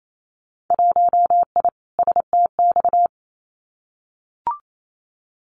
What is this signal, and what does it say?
Morse code, heard as a keyed tone, about 28 wpm, 700 Hz. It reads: 1I STX
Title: Winter Field Day Exchanges 28wpm